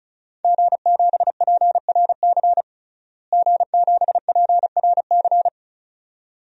Morse code, heard as a keyed tone, about 34 wpm, 700 Hz. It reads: G7PRC G7PRC
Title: DXCC-100 Call Signs - 5 Characters - Repeated 2x - 35wpm